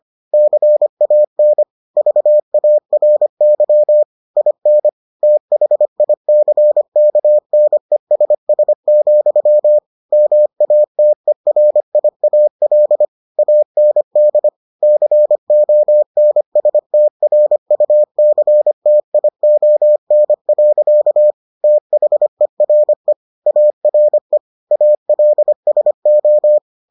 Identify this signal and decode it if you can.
Morse code, heard as a keyed tone, about 25 wpm, 600 Hz. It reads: CAN VARY IN THICKNESS, MATERIAL AND CONSTRUCTION. THERE ARE ALSO